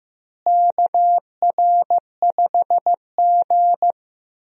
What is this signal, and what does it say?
Morse code, heard as a keyed tone, about 15 wpm, 700 Hz. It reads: KR5G
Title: USA Call Signs - 4 Characters - Repeated 2x - 15wpm